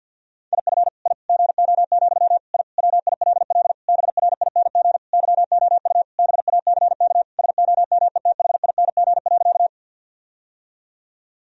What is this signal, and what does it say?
Morse code, heard as a keyed tone, about 50 wpm, 700 Hz. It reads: IF I GO, I WILL BRING YOU BACK SOMETHING.
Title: Sentences from Top 1000 Words - Review Part 3 - 50wpm